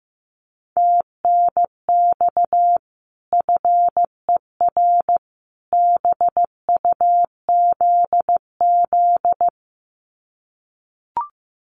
Morse code, heard as a keyed tone, about 14 wpm, 700 Hz. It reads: TNX FER BUZZ
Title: Common QSO Phrases 15wpm